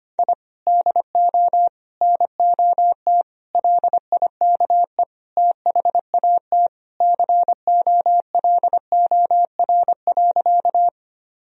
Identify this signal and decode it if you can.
Morse code, heard as a keyed tone, about 25 wpm, 700 Hz. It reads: I DO NOT LIKE THAT COLOR.
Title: Sentences from Top 1000 Words - Review Part 2 - 25wpm